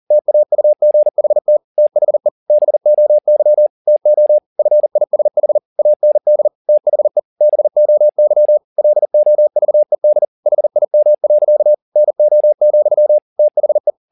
Morse code, heard as a keyed tone, about 40 wpm, 600 Hz. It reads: TAUGHT THE BOY TO FISH AND THE BOY LOVED HIM. NO, THE
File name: the old man and the sea 1 40wpm